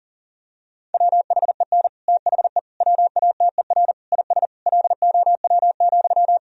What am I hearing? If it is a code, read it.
Morse code, heard as a keyed tone, about 40 wpm, 700 Hz. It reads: WHEN THE WATER IS LOW,
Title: Sentences from Top 1000 Words - Review Part 1 - 40wpm